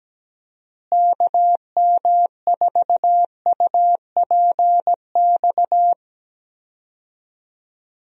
Morse code, heard as a keyed tone, about 17 wpm, 700 Hz. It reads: KM4UPX
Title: USA Call Signs - 6 Characters - Repeated 2x - 17wpm